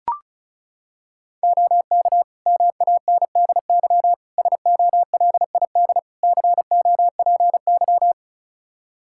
Morse code, heard as a keyed tone, about 36 wpm, 700 Hz. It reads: OK MANDY SOLID COPY
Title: Practice QSO - Exchange 3 - More Info - 35wpm